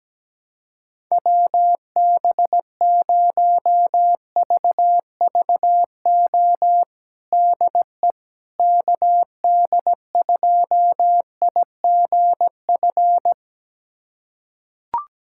Morse code, heard as a keyed tone, about 17 wpm, 700 Hz. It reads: WB0VVO DE KD2IGF
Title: USA Callsign DE Callsign - 6 Characters - 17wpm